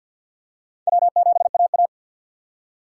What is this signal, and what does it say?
Morse code, heard as a keyed tone, about 52 wpm, 700 Hz. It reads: W7AA
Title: USA Call Signs - 4 Characters - 3x Character Spacing - 50wpm